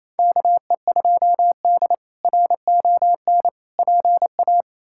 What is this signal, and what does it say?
Morse code, heard as a keyed tone, about 28 wpm, 700 Hz. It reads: KE2D RON PA
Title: K1USN SST QSO Practice - 28wpm